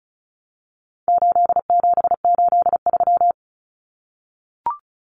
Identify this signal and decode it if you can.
Morse code, heard as a keyed tone, about 34 wpm, 700 Hz. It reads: 8783
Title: Common Cut Numbers - 4-Digit 35wpm